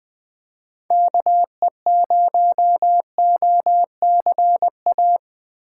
Morse code, heard as a keyed tone, about 20 wpm, 700 Hz. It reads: KE0OCA
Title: USA Call Signs - 6 Characters - 6x Character Spacing - 20wpm